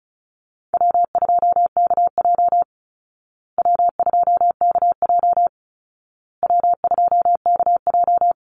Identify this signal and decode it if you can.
Morse code, heard as a keyed tone, about 34 wpm, 700 Hz. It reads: W2KJ W2KJ W2KJ
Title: Easy Call Signs Repeated 3x - 35wpm